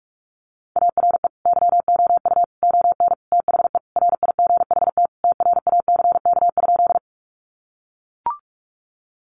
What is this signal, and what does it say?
Morse code, heard as a keyed tone, about 44 wpm, 700 Hz. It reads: ARE YOU ON THE RIGHT TRACK?
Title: Sentences from Top 1100 Words 45wpm